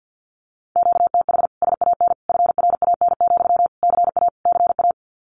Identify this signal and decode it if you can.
Morse code, heard as a keyed tone, about 50 wpm, 700 Hz. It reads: QTH SAN FRAN, CA CA
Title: Common QSO Phrases Encore 50wpm